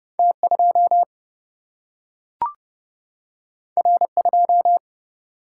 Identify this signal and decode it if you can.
Morse code, heard as a keyed tone, about 30 wpm, 700 Hz. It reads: T2 R2
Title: Single-Letter Single-Number 30wpm